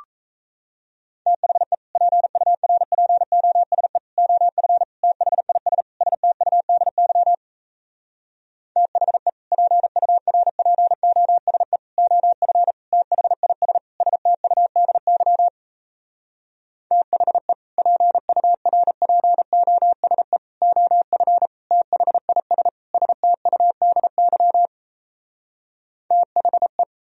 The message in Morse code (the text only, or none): THE PURPOSE OF THIS STUDY THE PURPOSE OF THIS STUDY THE PURPOSE OF THIS STUDY THE